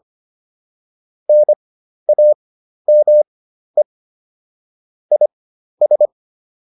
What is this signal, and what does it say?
Morse code, heard as a keyed tone, about 25 wpm, spaced as for 12 wpm, 600 Hz. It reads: NAME IS